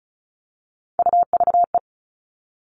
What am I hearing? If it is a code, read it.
Morse code, heard as a keyed tone, about 34 wpm, 700 Hz. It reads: UVE
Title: All Cut Numbers - 3-Digit 35wpm